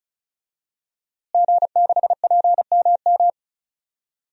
Morse code, heard as a keyed tone, about 34 wpm, 700 Hz. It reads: G6PMM
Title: DXCC-100 Call Signs - 5 Characters - Repeated 2x - 35wpm